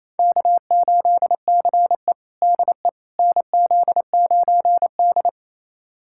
Morse code, heard as a keyed tone, about 28 wpm, 700 Hz. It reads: K8CE DE NZ9D
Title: USA Callsign DE Callsign - 4 Characters - 28wpm